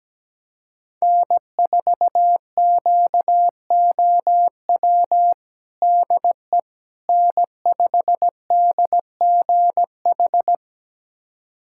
Morse code, heard as a keyed tone, about 17 wpm, 700 Hz. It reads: N4QOW DE N5DGH
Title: USA Callsign DE Callsign - 5 Characters - 17wpm